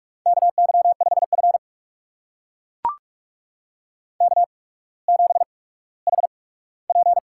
Morse code, heard as a keyed tone, about 44 wpm, spaced as for 19 wpm, 700 Hz. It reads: KY5F K7HP
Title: USA Call Signs - 4 Characters - 8x Character Spacing - 45wpm